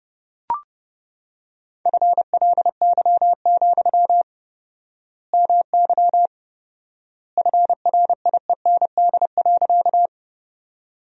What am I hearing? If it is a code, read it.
Morse code, heard as a keyed tone, about 30 wpm, 700 Hz. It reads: FLY, MY FRIEND.
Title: 4x Word-Spacing - Sentences from Top 500 Words - 30wpm